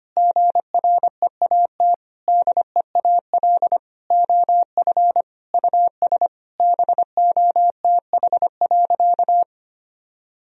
Morse code, heard as a keyed tone, about 25 wpm, 700 Hz. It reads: GREAT DEAL OF US BOTH.
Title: Sentences from Top 1000 Words - Review Part 4 - 25wpm